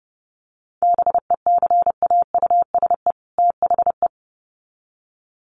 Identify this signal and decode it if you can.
Morse code, heard as a keyed tone, about 30 wpm, 700 Hz. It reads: BECAUSE THE
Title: Sets of 2 Words 30wpm